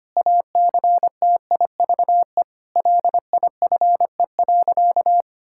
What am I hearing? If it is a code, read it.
Morse code, heard as a keyed tone, about 25 wpm, 700 Hz. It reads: ACTIVE LIFE.
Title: Sentences from Top 1000 Words - Review Part 6 - 25wpm